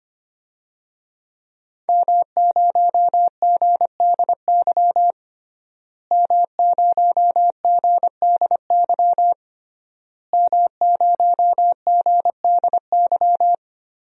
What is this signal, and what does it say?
Morse code, heard as a keyed tone, about 25 wpm, 700 Hz. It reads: M0GDY M0GDY M0GDY
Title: DXCC-100 Call Signs - 5 Characters - Repeated 3x - 25wpm